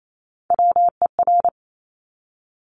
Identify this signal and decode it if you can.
Morse code, heard as a keyed tone, about 28 wpm, 700 Hz. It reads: WER
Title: Top Three Letter Combinations in Words 28wpm